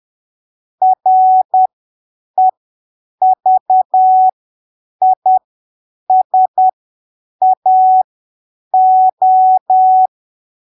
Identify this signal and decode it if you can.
Morse code, heard as a keyed tone, about 10 wpm, spaced as for 7 wpm, 750 Hz. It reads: REVISAO
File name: arquivos / texto / texto 22 10ppm